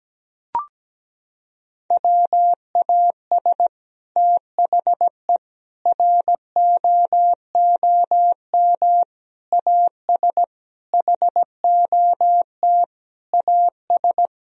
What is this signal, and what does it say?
Morse code, heard as a keyed tone, about 17 wpm, 700 Hz. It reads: WAS THE ROOM AS HOT AS
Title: Sentences from Top 1000 Words - Review Part 4 - 17wpm